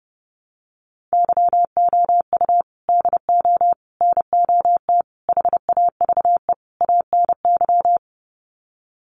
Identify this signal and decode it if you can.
Morse code, heard as a keyed tone, about 30 wpm, 700 Hz. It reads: YOU DO NOT HAVE ANY
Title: Sets of 5 Words Encore 30wpm